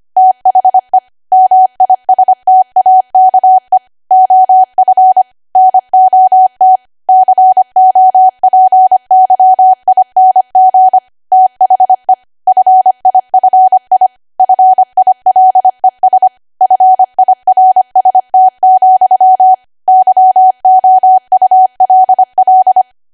Morse code, heard as a keyed tone, about 25 wpm, 750 Hz. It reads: THE MISTAKE OF NOT COPYING THE FIFI FILES FIRST, YOULL
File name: cwa / W1AW-Level3 / 25WPM